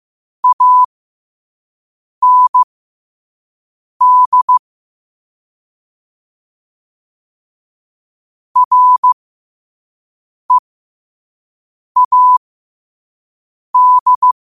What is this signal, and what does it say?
Morse code, heard as a keyed tone, about 15 wpm, spaced as for 5 wpm, 1000 Hz. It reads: AND READ
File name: qotdfiles / 5wpm